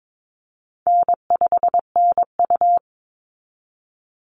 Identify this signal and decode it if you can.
Morse code, heard as a keyed tone, about 22 wpm, 700 Hz. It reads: N5NU
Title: Easy Call Signs Repeated 3x - 22wpm